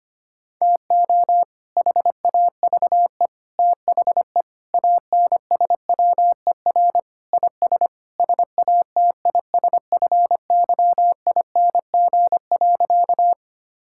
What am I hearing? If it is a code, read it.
Morse code, heard as a keyed tone, about 25 wpm, 700 Hz. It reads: TO HAVE THE ANSWER IS SATISFYING.
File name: practice / 25wpm / Sentences with 3-Word Phrases - 25wpm